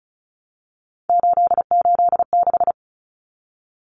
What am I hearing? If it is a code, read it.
Morse code, heard as a keyed tone, about 34 wpm, 700 Hz. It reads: 886
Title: Numbers - 3-Digit 35wpm